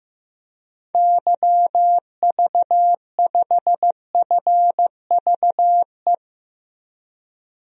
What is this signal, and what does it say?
Morse code, heard as a keyed tone, about 15 wpm, 700 Hz. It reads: YV5FVE
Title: DXCC-100 Call Signs - 6 Characters - Repeated 3x - 15wpm